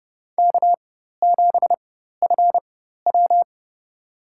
Morse code, heard as a keyed tone, about 30 wpm, spaced as for 14 wpm, 700 Hz. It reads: K7FW
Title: USA Call Signs - 4 Characters - 4x Character Spacing - 30wpm